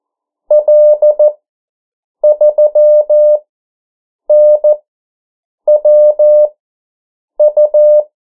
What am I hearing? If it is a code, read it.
Morse code, heard as a keyed tone, about 14 wpm, spaced as for 7 wpm, 600 Hz. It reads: L3NWU
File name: morse mp3 / 09 urok 8